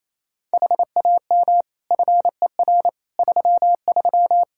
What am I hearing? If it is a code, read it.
Morse code, heard as a keyed tone, about 27 wpm, 700 Hz. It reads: HAM FER 33
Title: Practice QSO - Exchange 3 - More Info - 28wpm